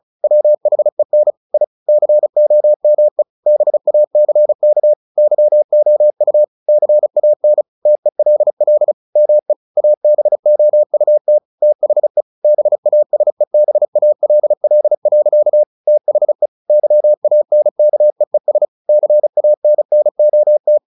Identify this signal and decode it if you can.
Morse code, heard as a keyed tone, about 34 wpm, 600 Hz. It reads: WHEN I COME BACK YOU CAN TELL ME ABOUT THE BASEBALL. THE YANKEES CANNOT
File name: the old man and the sea / the old man and the sea 1 35wpm